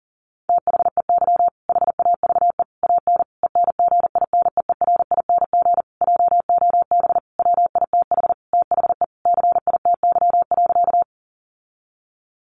Morse code, heard as a keyed tone, about 40 wpm, 700 Hz. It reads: THEY HAVE AN ENGINEERING JOB WITH THE CITY.
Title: Sentences from Top 1500 Words 40wpm